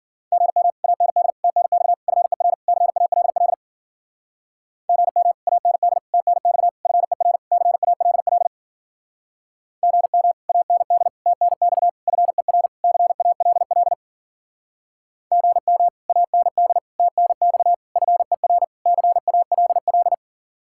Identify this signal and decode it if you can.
Morse code, heard as a keyed tone, about 52 wpm, 700 Hz. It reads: GM AND TNX FER CALL GM AND TNX FER CALL GM AND TNX FER CALL GM AND TNX FER CALL